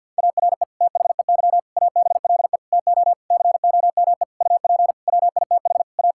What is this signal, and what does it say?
Morse code, heard as a keyed tone, about 50 wpm, 700 Hz. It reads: ARE THEY ABLE TO COME UP WITH A
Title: Sentences from Top 1000 Words - Review Part 5 - 50wpm